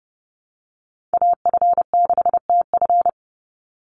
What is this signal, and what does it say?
Morse code, heard as a keyed tone, about 30 wpm, 700 Hz. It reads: AF6TF
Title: USA Call Signs 30wpm